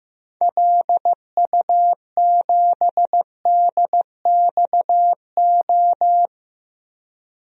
Morse code, heard as a keyed tone, about 15 wpm, 700 Hz. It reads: LU7DXO
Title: DXCC-100 Call Signs - 6 Characters - Repeated 2x - 15wpm